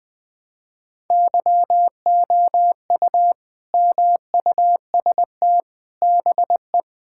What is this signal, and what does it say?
Morse code, heard as a keyed tone, about 21 wpm, 700 Hz. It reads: YOU MUST BE
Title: Sentences from Top 1600 Words 20wpm